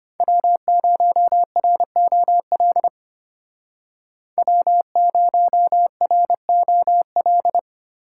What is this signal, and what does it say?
Morse code, heard as a keyed tone, about 33 wpm, 700 Hz. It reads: W0ROL W0ROL